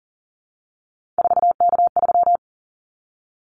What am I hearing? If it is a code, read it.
Morse code, heard as a keyed tone, about 40 wpm, 700 Hz. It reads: 4K3
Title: Callsign Prefixes for Top 100 DXCC - 40wpm